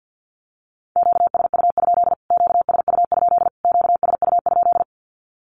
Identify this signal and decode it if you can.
Morse code, heard as a keyed tone, about 50 wpm, 700 Hz. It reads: QSU? QSU? QSU?
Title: Q-Codes Repeated 3x 50wpm